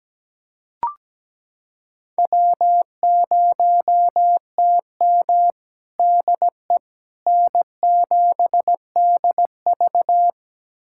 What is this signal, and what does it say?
Morse code, heard as a keyed tone, about 17 wpm, 700 Hz. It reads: W0TM DE N7DV
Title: USA Callsign DE Callsign - 4 Characters - 17wpm